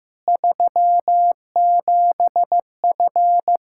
Morse code, heard as a keyed tone, about 15 wpm, 700 Hz. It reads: 37F
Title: K1USN SST QSO Practice 15wpm